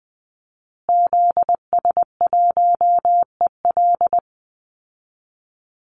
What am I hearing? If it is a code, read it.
Morse code, heard as a keyed tone, about 20 wpm, 700 Hz. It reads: ZS1EL
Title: Call Signs Repeated 2x - 20wpm